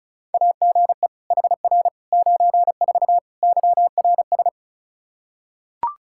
Morse code, heard as a keyed tone, about 36 wpm, 700 Hz. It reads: AGE HR 94 YRS
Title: Practice QSO - Exchange 3 - More Info - 35wpm